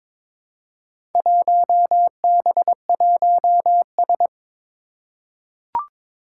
Morse code, heard as a keyed tone, about 22 wpm, 700 Hz. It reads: 1B1S
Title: Four Challenging Letter-Number Combinations 22wpm